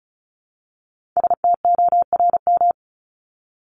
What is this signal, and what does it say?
Morse code, heard as a keyed tone, about 34 wpm, 700 Hz. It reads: STORM
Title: Top 1251-1500 Words 35wpm